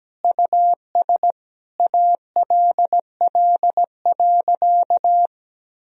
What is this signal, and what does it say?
Morse code, heard as a keyed tone, about 17 wpm, 700 Hz. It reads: US ALL.
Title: Sentences from Top 1000 Words - Review Part 1 - 17wpm